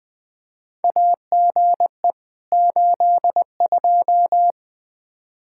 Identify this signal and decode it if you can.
Morse code, heard as a keyed tone, about 20 wpm, 700 Hz. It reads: AGE 82
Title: Practice QSO - Exchange 3 - More Info - 20wpm